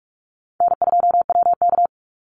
Morse code, heard as a keyed tone, about 44 wpm, 700 Hz. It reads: N2WK
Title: Call Signs Repeated 2x - 45wpm